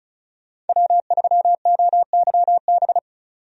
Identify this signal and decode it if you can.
Morse code, heard as a keyed tone, about 34 wpm, 700 Hz. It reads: W3OYB
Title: USA Call Signs - 5 Characters - 35wpm